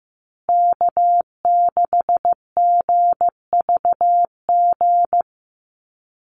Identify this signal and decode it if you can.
Morse code, heard as a keyed tone, about 15 wpm, 700 Hz. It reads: K6GVG
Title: Call Signs Repeated 3x - 15wpm